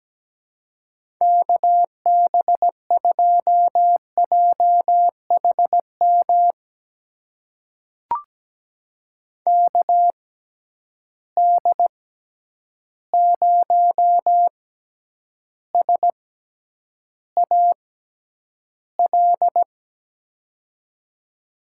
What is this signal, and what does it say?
Morse code, heard as a keyed tone, about 17 wpm, spaced as for 9 wpm, 700 Hz. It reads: KB2JHM KD0SAL
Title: USA Call Signs - 6 Characters - 6x Character Spacing - 17wpm